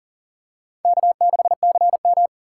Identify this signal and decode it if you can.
Morse code, heard as a keyed tone, about 40 wpm, 700 Hz. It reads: K6CM
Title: USA Call Signs - 4x Character Spacing - 40wpm